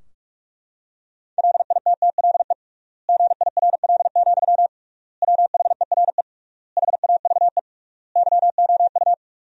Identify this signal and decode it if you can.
Morse code, heard as a keyed tone, about 42 wpm, 700 Hz. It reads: LITTLE GIRL, WHERE HAVE YOU
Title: 3x Word-Spacing - Sentences from Top 400 Words - 45wpm